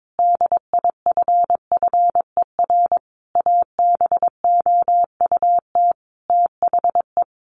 Morse code, heard as a keyed tone, about 22 wpm, 700 Hz. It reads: DIFFER ABOUT THE
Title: Sentences from Top 1400 Words 22wpm